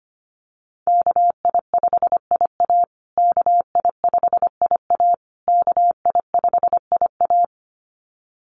Morse code, heard as a keyed tone, about 25 wpm, 700 Hz. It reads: KI5IA KI5IA KI5IA
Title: USA Call Signs Repeated 3x 25wpm